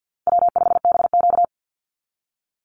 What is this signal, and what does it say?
Morse code, heard as a keyed tone, about 50 wpm, 700 Hz. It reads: W5BQ
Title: Easy Call Signs - 50wpm